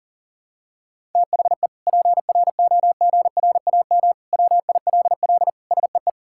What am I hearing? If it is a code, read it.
Morse code, heard as a keyed tone, about 37 wpm, 700 Hz. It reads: THE PROGRAM WILL SEE
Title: Sentences from Top 1000 Words - Review Part 5 - 40wpm